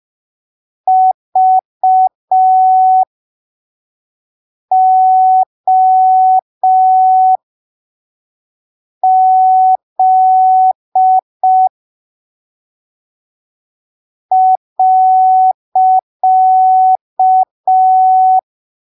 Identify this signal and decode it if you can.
Morse code, heard as a keyed tone, about 5 wpm, spaced as for 3 wpm, 750 Hz. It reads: VOZ .
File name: arquivos / texto / texto 04 5ppm